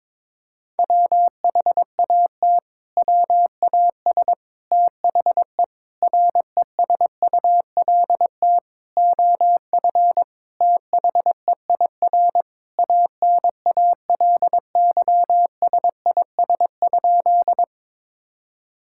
Morse code, heard as a keyed tone, about 22 wpm, 700 Hz. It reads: WHAT WAS THE RESULT OF THEIR ANALYSIS?
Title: Sentences from Top 1000 Words - Review Part 5 - 22wpm